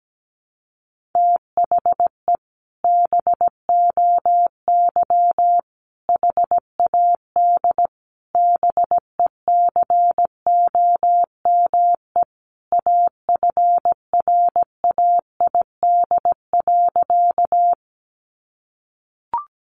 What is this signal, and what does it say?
Morse code, heard as a keyed tone, about 17 wpm, 700 Hz. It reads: THE BOY HAD BECOME AFRAID.
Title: Sentences from Top 1500 Words 17wpm